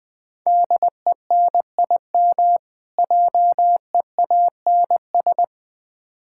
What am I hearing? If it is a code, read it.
Morse code, heard as a keyed tone, about 21 wpm, 690 Hz. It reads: DENIM JEANS
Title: Difficult Words with Hints - 20wpm